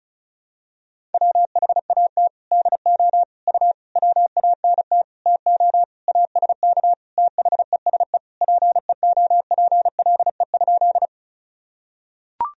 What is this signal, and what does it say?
Morse code, heard as a keyed tone, about 34 wpm, 690 Hz. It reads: WHAT DO U WANT TO ASK THESE PEOPLE?
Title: Sentences from Top 200 Words and Abbreviations 35wpm